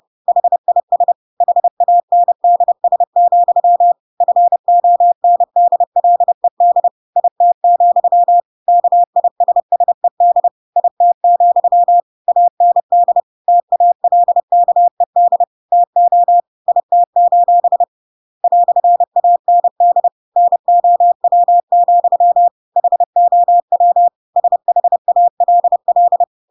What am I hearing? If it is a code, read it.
Morse code, heard as a keyed tone, about 30 wpm, 700 Hz. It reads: HIS HANDS, FONDLED IT, KISSED IT, AND TALKED TO IT: "AND NOW, HOW SHALL
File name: Pinocchio / Pinocchio ch5 30 30 ik1hge